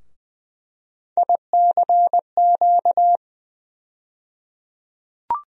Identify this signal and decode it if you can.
Morse code, heard as a keyed tone, about 20 wpm, 700 Hz. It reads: ICQ
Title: USA Call Sign Postfixes - 3 Letters 20wpm